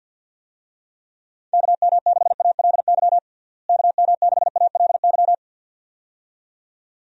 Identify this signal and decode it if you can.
Morse code, heard as a keyed tone, about 48 wpm, 700 Hz. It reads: KM6ALY KM6ALY
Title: USA Call Signs - 6 Characters - Repeated 2x - 50wpm